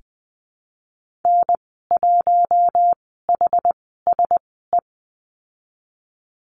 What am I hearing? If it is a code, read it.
Morse code, heard as a keyed tone, about 20 wpm, 700 Hz. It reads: N 1 H S E
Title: 076 Lesson 13g - Letter L - Call Signs 20wpm